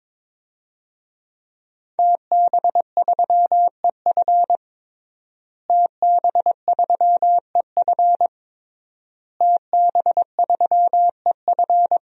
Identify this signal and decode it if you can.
Morse code, heard as a keyed tone, about 22 wpm, 700 Hz. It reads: TB3EF TB3EF TB3EF
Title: DXCC-100 Call Signs - 5 Characters - Repeated 3x - 22wpm